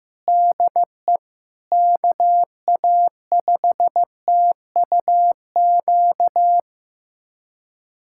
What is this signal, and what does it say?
Morse code, heard as a keyed tone, about 15 wpm, 700 Hz. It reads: DE KA5TUQ
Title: USA Callsign DE Callsign - 6 Characters - 15wpm